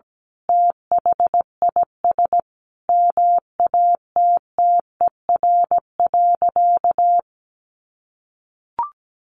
Morse code, heard as a keyed tone, about 17 wpm, 700 Hz. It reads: THIS MATTER.
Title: Sentences from Top 1500 Words 17wpm